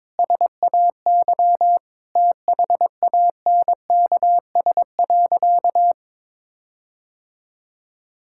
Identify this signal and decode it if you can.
Morse code, heard as a keyed tone, about 22 wpm, 700 Hz. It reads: SAY THANKS.
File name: practice / 22wpm / Sentences with 4-Word Phrases - 22wpm